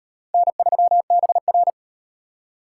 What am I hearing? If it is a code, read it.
Morse code, heard as a keyed tone, about 33 wpm, 700 Hz. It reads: N3BR